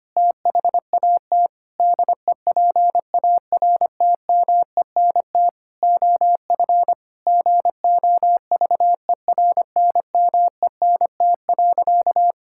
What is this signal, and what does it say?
Morse code, heard as a keyed tone, about 25 wpm, 700 Hz. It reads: THAT DEPARTMENT OF GOVERNMENT.
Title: Sentences from Top 1000 Words - Review Part 5 - 25wpm